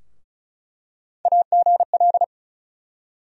Morse code, heard as a keyed tone, about 34 wpm, 700 Hz. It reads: AGL
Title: USA Call Sign Postfixes - 3 Letters 35wpm